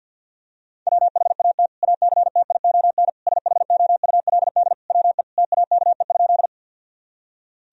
Morse code, heard as a keyed tone, about 47 wpm, 700 Hz. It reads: WHAT ACTION SHOULD WE TAKE?
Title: Sentences from Top 1000 Words - Review Part 4 - 50wpm